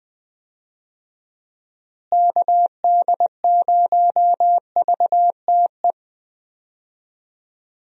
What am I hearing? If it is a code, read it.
Morse code, heard as a keyed tone, about 20 wpm, 700 Hz. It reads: KD0VTE
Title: USA Call Signs - 6 Characters - Repeated 3x - 20wpm